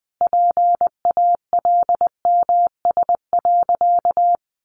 Morse code, heard as a keyed tone, about 20 wpm, 700 Hz. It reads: PALMS.
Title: A Sound Of Thunder 20wpm